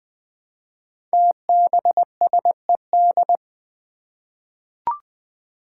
Morse code, heard as a keyed tone, about 20 wpm, 700 Hz. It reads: TBSED
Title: Five Challenging Letter-Number Combinations 20wpm